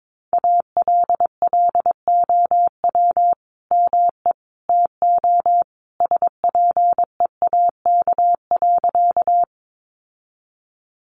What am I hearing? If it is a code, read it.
Morse code, heard as a keyed tone, about 22 wpm, 700 Hz. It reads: ALLOW ME TO SPEAK.
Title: Sentences from Top 1100 Words 22wpm